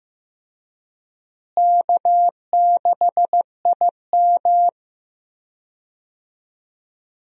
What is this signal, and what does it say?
Morse code, heard as a keyed tone, about 15 wpm, 700 Hz. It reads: K6IM
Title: USA Call Signs - 4 Characters - Repeated 3x - 15wpm